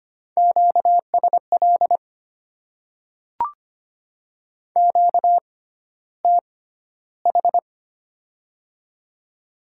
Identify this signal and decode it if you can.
Morse code, heard as a keyed tone, about 25 wpm, spaced as for 13 wpm, 700 Hz. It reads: QSL QTH
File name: Top 90 QSO Elements - 6x Character Spacing - 25wpm